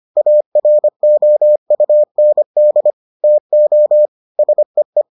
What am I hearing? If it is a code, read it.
Morse code, heard as a keyed tone, about 25 wpm, 600 Hz. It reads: AROUND TO SEE